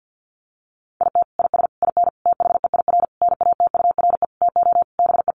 Morse code, heard as a keyed tone, about 49 wpm, 700 Hz. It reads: IT IS IN THEIR NATURE TO BE
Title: Sentences from Top 1300 Words 50wpm